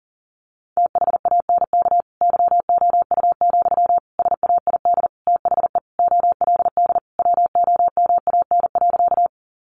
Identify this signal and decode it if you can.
Morse code, heard as a keyed tone, about 40 wpm, 700 Hz. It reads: THANK YOU, SAID THE OLD WOMAN.
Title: Sentences from Top 1100 Words 40wpm